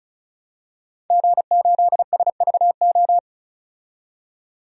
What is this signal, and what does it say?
Morse code, heard as a keyed tone, about 34 wpm, 700 Hz. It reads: G8SVO
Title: DXCC-100 Call Signs - 5 Characters - 35wpm